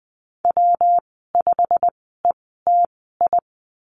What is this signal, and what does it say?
Morse code, heard as a keyed tone, about 20 wpm, spaced as for 14 wpm, 700 Hz. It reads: W5ETI
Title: 140 Lesson 23g - Letter f - Call Signs 20wpm